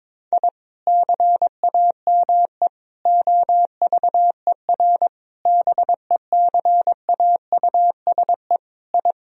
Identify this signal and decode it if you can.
Morse code, heard as a keyed tone, about 22 wpm, 700 Hz. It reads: I CAME OVER BECAUSE I
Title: Sentences from Top 1000 Words - Review Part 3 - 22wpm